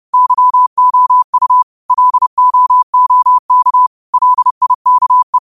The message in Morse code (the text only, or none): YOU LOOK LIKE